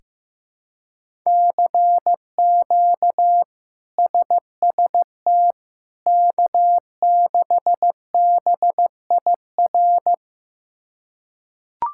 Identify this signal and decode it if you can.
Morse code, heard as a keyed tone, about 15 wpm, 700 Hz. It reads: CQ SST K6BIR
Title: K1USN SST QSO Practice - 15wpm